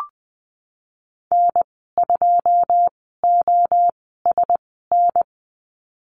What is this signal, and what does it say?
Morse code, heard as a keyed tone, about 20 wpm, spaced as for 14 wpm, 700 Hz. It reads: N2OSN
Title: 088 Lesson 15g - Number 5 - Call Signs 20wpm